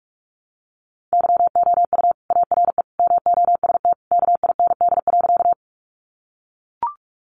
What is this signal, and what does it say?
Morse code, heard as a keyed tone, about 44 wpm, 700 Hz. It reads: YOU ARE MOST KIND.
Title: Sentences from Top 200 Words 45wpm